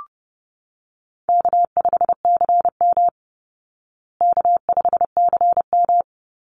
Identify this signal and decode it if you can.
Morse code, heard as a keyed tone, about 30 wpm, 700 Hz. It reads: K5CM K5CM
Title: Easy Call Signs Repeated 2x - 30wpm